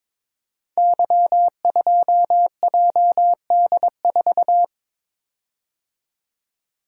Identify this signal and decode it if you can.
Morse code, heard as a keyed tone, about 22 wpm, 700 Hz. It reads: Y2JD4
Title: Code Groups - 5 Letter:Numbers - 22wpm